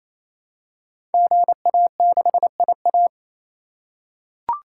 Morse code, heard as a keyed tone, about 27 wpm, 700 Hz. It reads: GA6IA
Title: Code Groups - 5 Letter:Numbers - 28wpm